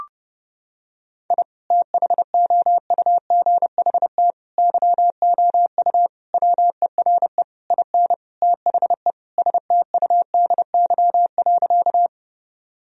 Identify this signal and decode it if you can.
Morse code, heard as a keyed tone, about 29 wpm, 700 Hz. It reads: I THOUGHT YOU WERE IN THE STUDY.
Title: Sentences from Top 1000 Words - Review Part 1 - 30wpm